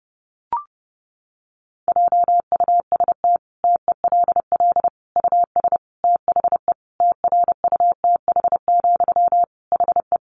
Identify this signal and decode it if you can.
Morse code, heard as a keyed tone, about 30 wpm, 700 Hz. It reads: JUST TELL US THE TRUTH, HE
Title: Sentences from Top 800 Words 30wpm